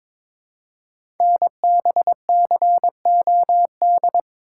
Morse code, heard as a keyed tone, about 22 wpm, 700 Hz. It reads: NBCOD
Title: Code Groups - 5 Letters - 22wpm